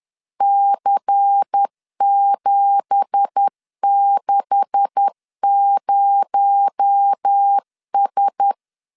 Morse code, heard as a keyed tone, about 10 wpm, 800 Hz. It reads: C760S
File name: LEZIONI / 9 2867 10wpm 2733